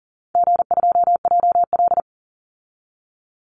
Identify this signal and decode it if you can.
Morse code, heard as a keyed tone, about 40 wpm, 700 Hz. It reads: G2JL
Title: Easy Call Signs Repeated 3x - 40wpm